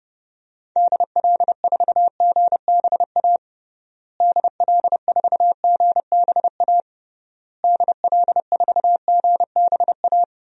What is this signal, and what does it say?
Morse code, heard as a keyed tone, about 30 wpm, 700 Hz. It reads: DL4GBA DL4GBA DL4GBA
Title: DXCC-100 Call Signs - 6 Characters - Repeated 3x - 30wpm